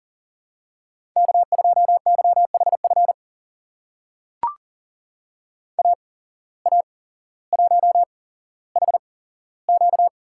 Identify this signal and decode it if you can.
Morse code, heard as a keyed tone, about 40 wpm, spaced as for 17 wpm, 700 Hz. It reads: K2YHF AA1HQ
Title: USA Call Signs - 5 Characters - 8x Character Spacing - 40wpm